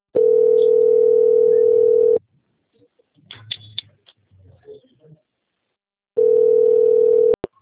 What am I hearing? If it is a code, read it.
Ringback tone (North America), 440 + 480 Hz, with speech.
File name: out- - - - - .1303